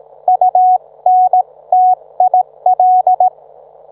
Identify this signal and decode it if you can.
Morse code, heard as a keyed tone, about 18 wpm, 740 Hz. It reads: UNTIL